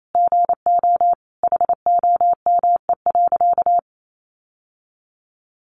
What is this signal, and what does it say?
Morse code, heard as a keyed tone, about 28 wpm, 700 Hz. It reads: GO HOME.
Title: Sentences from Top 1400 Words 28wpm